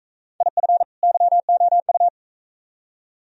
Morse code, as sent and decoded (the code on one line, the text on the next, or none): .. ..-. / -.-- --- ..-
IF YOU